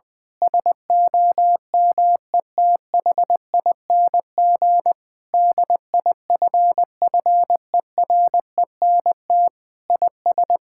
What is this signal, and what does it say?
Morse code, heard as a keyed tone, about 20 wpm, 700 Hz. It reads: SOMETHING DIFFERENT IS
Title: Sentences with 8-Word Phrases - 20wpm